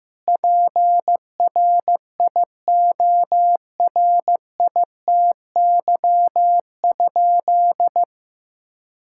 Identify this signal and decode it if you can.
Morse code, heard as a keyed tone, about 16 wpm, 700 Hz. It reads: PRIORITY?
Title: Sentences from Top 1600 Words 15wpm